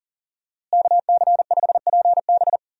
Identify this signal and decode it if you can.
Morse code, heard as a keyed tone, about 40 wpm, 700 Hz. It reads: KC5PB
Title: USA Call Signs - 5 Characters - Repeated 3x - 40wpm